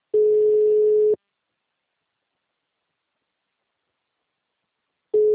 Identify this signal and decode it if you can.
Ringback tone (Europe, 425 Hz).
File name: out- -2039- - - .14552